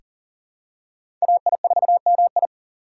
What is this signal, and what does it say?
Morse code, heard as a keyed tone, about 40 wpm, 700 Hz. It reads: AI4MI
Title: K1USN SST QSO Practice - 40wpm